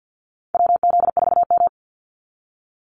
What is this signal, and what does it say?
Morse code, heard as a keyed tone, about 50 wpm, 700 Hz. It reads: WZ4M
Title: NA SOTA Activator Callsigns 50wpm